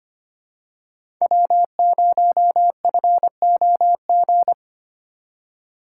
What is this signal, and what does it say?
Morse code, heard as a keyed tone, about 25 wpm, 700 Hz. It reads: W0FOG
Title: USA Call Signs - 5 Characters - 2x Character Spacing - 25wpm